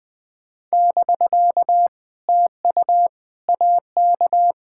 Morse code, heard as a keyed tone, about 20 wpm, 700 Hz. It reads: <BK> TU AK
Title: POTA QSO Practice - 20wpm